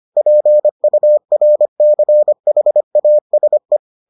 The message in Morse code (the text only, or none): PURCHASE